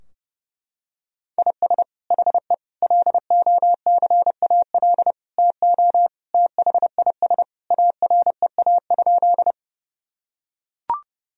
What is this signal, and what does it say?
Morse code, heard as a keyed tone, about 30 wpm, 700 Hz. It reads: IS HE LOCAL TO THIS AREA?
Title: Sentences from Top 1000 Words - Review Part 3 - 30wpm